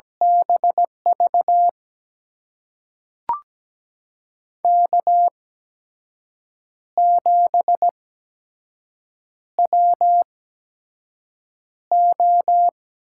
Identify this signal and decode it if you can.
Morse code, heard as a keyed tone, about 17 wpm, spaced as for 6 wpm, 700 Hz. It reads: BV K7WO
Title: USA Call Signs - 8x Character Spacing - 17wpm